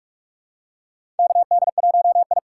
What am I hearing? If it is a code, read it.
Morse code, heard as a keyed tone, about 47 wpm, 700 Hz. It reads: KD1I
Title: USA Call Signs - 4 Characters - 3x Character Spacing - 45wpm